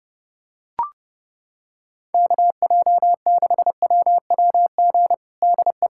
Morse code, heard as a keyed tone, about 30 wpm, 700 Hz. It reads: KJ6WWG DE
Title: USA Callsign DE Callsign - 6 Characters - 30wpm